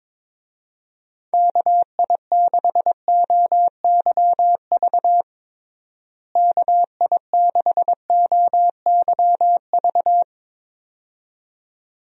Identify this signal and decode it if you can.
Morse code, heard as a keyed tone, about 22 wpm, 700 Hz. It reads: KI6OYV KI6OYV
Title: USA Call Signs - 6 Characters - Repeated 2x - 22wpm